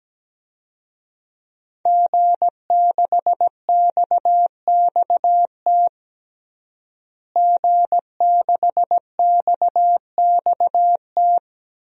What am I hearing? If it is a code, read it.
Morse code, heard as a keyed tone, about 17 wpm, 700 Hz. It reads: G6XXT G6XXT
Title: DXCC-100 Call Signs - 5 Characters - Repeated 2x - 17wpm